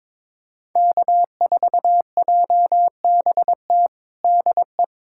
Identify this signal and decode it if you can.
Morse code, heard as a keyed tone, about 22 wpm, 700 Hz. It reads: K4JBT DE
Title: USA Callsign DE Callsign - 5 Characters - 22wpm